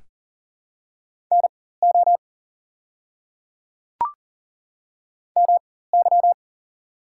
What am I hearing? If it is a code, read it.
Morse code, heard as a keyed tone, about 40 wpm, spaced as for 19 wpm, 700 Hz. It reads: NO MY
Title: 4x Character Spacing - Top 300 Words - 40wpm